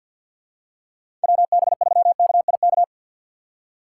Morse code, heard as a keyed tone, about 53 wpm, 700 Hz. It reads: WB3KIK
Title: USA Call Signs - 6 Characters - 4x Character Spacing - 50wpm